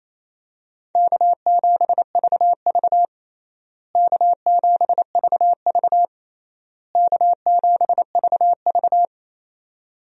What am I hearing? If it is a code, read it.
Morse code, heard as a keyed tone, about 28 wpm, 700 Hz. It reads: K7VV K7VV K7VV
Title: USA Call Signs - 4 Characters - Repeated 3x - 28wpm